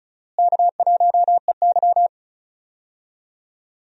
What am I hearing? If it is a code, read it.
Morse code, heard as a keyed tone, about 36 wpm, 700 Hz. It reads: K1EY
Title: USA Call Signs - 4 Characters - Repeated 2x - 35wpm